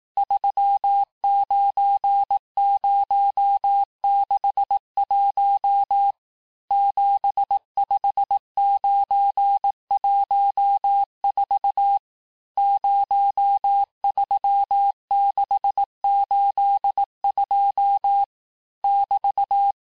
Morse code, heard as a keyed tone, about 18 wpm, 800 Hz. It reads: 39061 75914 03682 <BT>
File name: cours-du-lundi files / cours-1852-texte-chiffre-vitesse-17MPM